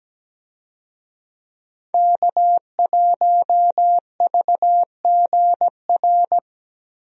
Morse code, heard as a keyed tone, about 17 wpm, 700 Hz. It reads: K1VGR
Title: USA Call Signs - 5 Characters - Repeated 2x - 17wpm